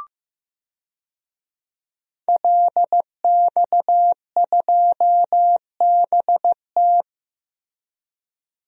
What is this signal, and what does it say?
Morse code, heard as a keyed tone, about 15 wpm, 700 Hz. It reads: LX2BT
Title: DXCC-100 Call Signs - 5 Characters - Repeated 2x - 15wpm